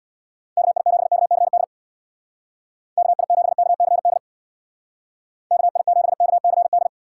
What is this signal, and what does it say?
Morse code, heard as a keyed tone, about 60 wpm, 700 Hz. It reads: KI7KCD KI7KCD KI7KCD
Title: Speed Racing - USA 2x3 Call Signs 50wpm